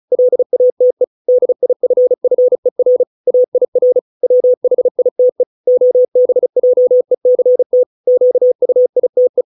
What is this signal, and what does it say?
Morse code, heard as a keyed tone, about 34 wpm, 500 Hz. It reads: LATE DIFFER AIR WHITE OBJECT QUITE